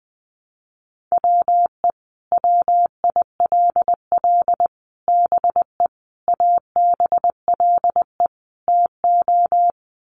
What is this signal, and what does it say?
Morse code, heard as a keyed tone, about 20 wpm, 700 Hz. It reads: WE WILL BE ABLE TO
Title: Sets of 5 Words Encore 20wpm